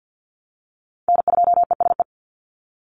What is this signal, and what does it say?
Morse code, heard as a keyed tone, about 49 wpm, 700 Hz. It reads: N2ESE
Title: NA SOTA Chaser Callsigns 50wpm